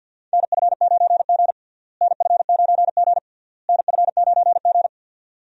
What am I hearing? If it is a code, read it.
Morse code, heard as a keyed tone, about 52 wpm, 700 Hz. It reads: NF9G NF9G NF9G
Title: USA Call Signs - 4 Characters - Repeated 3x - 50wpm